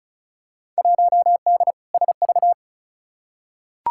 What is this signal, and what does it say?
Morse code, heard as a keyed tone, about 34 wpm, 700 Hz. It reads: 1D SV
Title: Field Day Exchanges - 35wpm